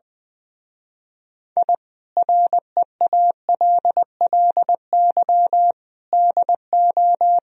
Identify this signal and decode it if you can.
Morse code, heard as a keyed tone, about 21 wpm, 700 Hz. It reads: I REALLY DO
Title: Speed-Racing - Sets of 4 Words 20wpm